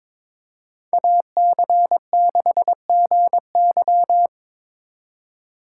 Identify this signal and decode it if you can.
Morse code, heard as a keyed tone, about 25 wpm, 700 Hz. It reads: AC6GY
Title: Speed Racing - USA 2x2 Call Signs 22wpm